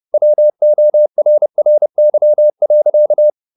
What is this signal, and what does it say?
Morse code, heard as a keyed tone, about 30 wpm, 600 Hz. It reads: WORRY.